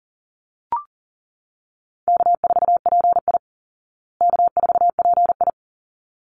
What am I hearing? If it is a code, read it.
Morse code, heard as a keyed tone, about 40 wpm, 700 Hz. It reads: K4PI K4PI
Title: Call Signs Repeated 3x - 40wpm